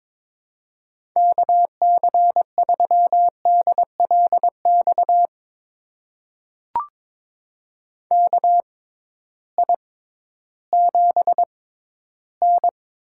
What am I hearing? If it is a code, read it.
Morse code, heard as a keyed tone, about 22 wpm, spaced as for 13 wpm, 700 Hz. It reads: KC3DLX KI7N
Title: USA Call Signs - 6 Characters - 6x Character Spacing - 22wpm